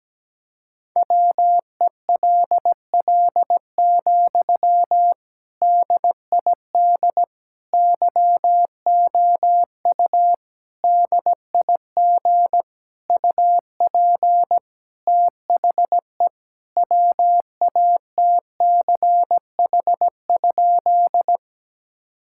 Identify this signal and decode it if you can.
Morse code, heard as a keyed tone, about 17 wpm, 700 Hz. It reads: WELL, DID YOU DIG UP THE WATCH?
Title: Sentences from Top 1600 Words 17wpm